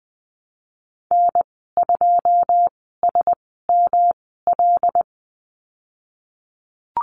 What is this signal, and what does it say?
Morse code, heard as a keyed tone, about 20 wpm, spaced as for 15 wpm, 700 Hz. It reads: N2SML
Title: 128 Lesson 21g - Number 6 - Call Signs 20wpm